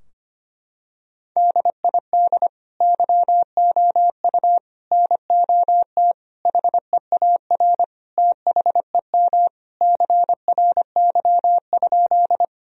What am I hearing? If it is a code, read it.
Morse code, heard as a keyed tone, about 25 wpm, 700 Hz. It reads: DID YOU NOT HEAR THEM CRY?
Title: Sentences from Top 1000 Words - Review Part 2 - 25wpm